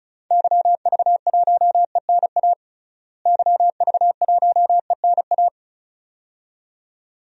Morse code, heard as a keyed tone, about 34 wpm, 700 Hz. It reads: YV1ENA YV1ENA
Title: DXCC-100 Call Signs - 6 Characters - Repeated 2x - 35wpm